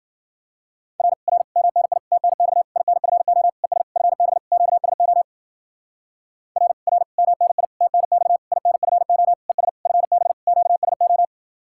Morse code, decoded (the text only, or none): R R MNI TNX INFO ES FB QSO R R MNI TNX INFO ES FB QSO